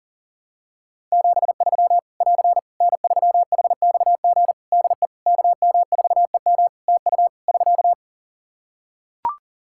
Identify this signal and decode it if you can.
Morse code, heard as a keyed tone, about 40 wpm, 700 Hz. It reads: 73 <AR> N3HXG DE KM4EM TU <SK>
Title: Practice QSO - Exchange 4 - Ending - 40wpm